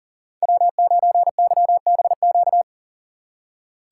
Morse code, heard as a keyed tone, about 40 wpm, 700 Hz. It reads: W9YBQ
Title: USA Call Signs - 5 Characters - 2x Character Spacing - 40wpm